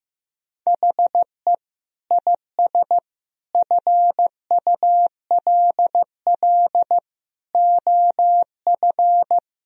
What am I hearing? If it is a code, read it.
Morse code, heard as a keyed tone, about 15 wpm, 700 Hz. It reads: HE IS FULL OF
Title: Sentences from Top 1000 Words - Review Part 3 - 15wpm